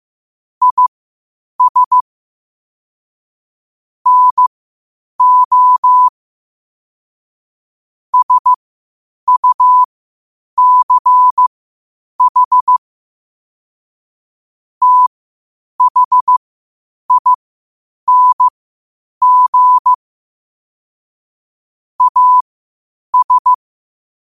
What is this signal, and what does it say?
Morse code, heard as a keyed tone, about 15 wpm, spaced as for 8 wpm, 1000 Hz. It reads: IS NO SUCH THING AS
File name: qotdfiles / 8wpm